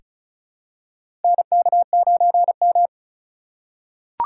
Morse code, heard as a keyed tone, about 36 wpm, 700 Hz. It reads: NK9M
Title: USA Call Signs - 4x Character Spacing - 35wpm